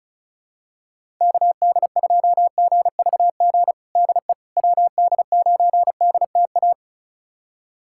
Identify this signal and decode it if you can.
Morse code, heard as a keyed tone, about 36 wpm, 700 Hz. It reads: KD2GVG DE WD9DTA
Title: USA Callsign DE Callsign - 6 Characters - 35wpm